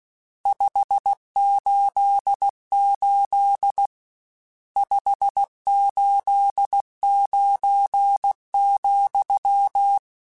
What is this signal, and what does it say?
Morse code, heard as a keyed tone, about 16 wpm, 800 Hz. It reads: 588 589,
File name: RSTs / RST-mp3s / Various RST @15